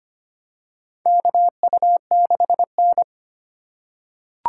Morse code, heard as a keyed tone, about 25 wpm, 700 Hz. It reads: KU6N
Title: USA Call Signs - 4 Characters - 4x Character Spacing - 25wpm